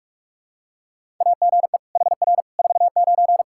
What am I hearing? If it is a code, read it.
Morse code, heard as a keyed tone, about 46 wpm, 700 Hz. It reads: AGE HR 49
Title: Practice QSO - Exchange 3 - More Info - 45wpm